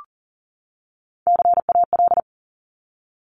Morse code, heard as a keyed tone, about 42 wpm, 690 Hz. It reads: CAL
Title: Top Three Letter Combinations in Words 40wpm